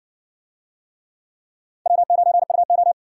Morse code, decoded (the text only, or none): W9UO